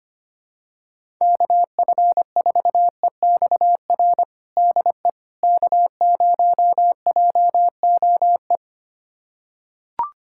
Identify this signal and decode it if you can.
Morse code, heard as a keyed tone, about 25 wpm, 700 Hz. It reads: KF4EXR DE K0JOE
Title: USA Callsign DE Callsign 25wpm